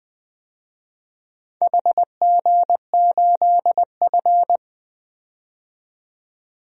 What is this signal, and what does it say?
Morse code, heard as a keyed tone, about 20 wpm, 700 Hz. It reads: HG8F
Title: DXCC-100 Call Signs - 4 Characters - 20wpm